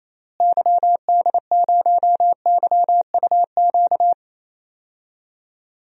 Morse code, heard as a keyed tone, about 28 wpm, 700 Hz. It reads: YD0YUQ
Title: DXCC-100 Call Signs - 6 Characters - Repeated 2x - 28wpm